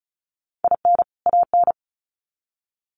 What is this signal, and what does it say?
Morse code, heard as a keyed tone, about 34 wpm, 700 Hz. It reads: IN AN
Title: Sets of 2 Words 35wpm